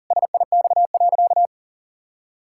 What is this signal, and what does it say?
Morse code, heard as a keyed tone, about 40 wpm, 700 Hz. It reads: SIX.
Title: Pangram Challenge v2 - 40wpm